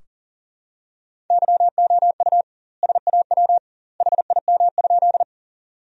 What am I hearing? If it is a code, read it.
Morse code, heard as a keyed tone, about 40 wpm, 700 Hz. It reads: YOU SAW HIM?
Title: 2x Word-Spacing - Sentences from Top 300 Words - 40wpm